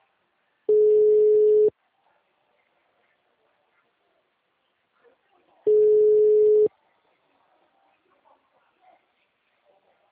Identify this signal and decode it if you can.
Ringback tone (Europe, 425 Hz).